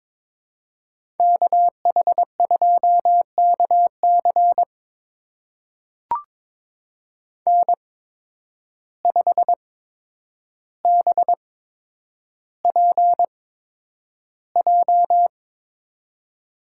Morse code, heard as a keyed tone, about 22 wpm, spaced as for 9 wpm, 700 Hz. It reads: KH2KC N5BPJ
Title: USA Call Signs - 5 Characters - 8x Character Spacing - 22wpm